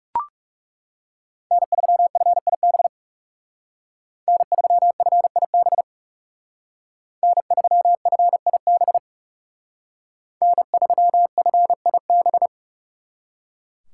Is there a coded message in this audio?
Yes: Morse code from 1.5 to 12.5 s, heard as a keyed tone, about 38 wpm, 700 Hz. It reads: N3FIB N3FIB N3FIB N3FIB